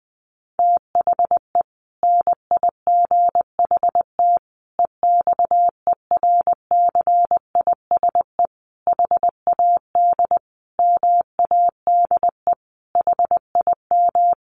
Morse code, heard as a keyed tone, about 20 wpm, 700 Hz. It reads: THE NIGHT EXERCISE HAD MADE HIM
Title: Sentences from Top 1400 Words 20wpm